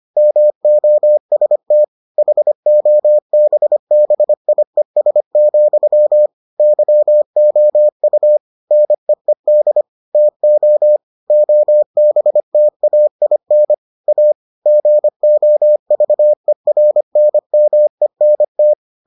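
Morse code, heard as a keyed tone, about 25 wpm, 600 Hz. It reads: MOST HOBBIES, YOU NEED TO OBTAIN A GOVERNMENT
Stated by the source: Morse edition of the Foundations of Amateur Radio episodes.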